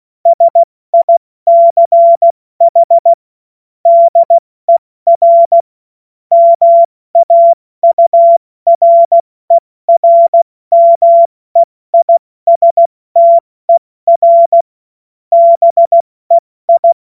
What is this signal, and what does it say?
Morse code, heard as a keyed tone, about 16 wpm, 680 Hz. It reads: SICH DER MAURERMEISTER BEI
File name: Morsekurs / Witze / Witz 09. Dezember 2025 70 F80